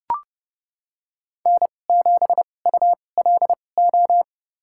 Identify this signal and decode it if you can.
Morse code, heard as a keyed tone, about 30 wpm, spaced as for 22 wpm, 700 Hz. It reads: N7ULO
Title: USA Call Signs - 5 Characters - 2x Character Spacing - 30wpm